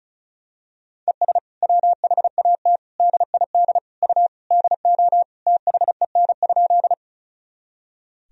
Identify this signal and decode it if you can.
Morse code, heard as a keyed tone, about 34 wpm, 700 Hz. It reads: ES WHAT DID U DO THEN?
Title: Sentences from Top 100 Words and Abbreviations 35wpm